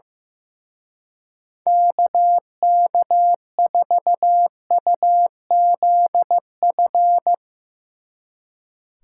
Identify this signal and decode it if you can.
Morse code, heard as a keyed tone, about 16 wpm, 700 Hz. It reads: KK4UZF
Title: Speed Racing - USA Call Signs 15wpm